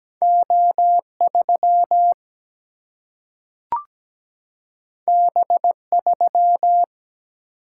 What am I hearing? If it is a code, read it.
Morse code, heard as a keyed tone, about 17 wpm, 700 Hz. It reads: O3 B3
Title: Single-Letter Single-Number 17wpm